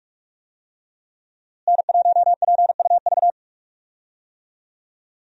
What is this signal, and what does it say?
Morse code, heard as a keyed tone, about 43 wpm, 700 Hz. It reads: N1PUV
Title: USA Call Signs - 5 Characters - Repeated 2x - 45wpm